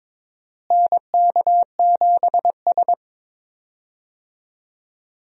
Morse code, heard as a keyed tone, about 22 wpm, 700 Hz. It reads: NK7S
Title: USA Call Signs - 4 Characters - Repeated 3x - 22wpm